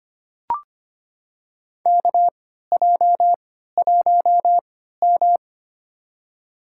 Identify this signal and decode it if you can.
Morse code, heard as a keyed tone, about 25 wpm, spaced as for 14 wpm, 700 Hz. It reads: KJ1M
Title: USA Call Signs - 3x Character Spacing - 25wpm